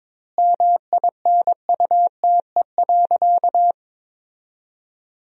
Morse code, heard as a keyed tone, about 22 wpm, 700 Hz. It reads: MINUTE.
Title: 2x Word-Spacing - Sentences from Top 500 Words - 22wpm